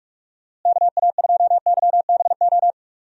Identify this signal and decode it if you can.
Morse code, heard as a keyed tone, about 46 wpm, 700 Hz. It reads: KA2YBO
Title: USA Call Signs - 6 Characters - Repeated 2x - 45wpm